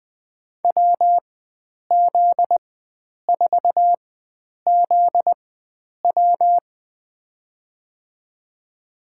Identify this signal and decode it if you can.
Morse code, heard as a keyed tone, about 20 wpm, spaced as for 9 wpm, 700 Hz. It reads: WZ4ZW
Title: USA Call Signs - 5 Characters - 4x Character Spacing - 20wpm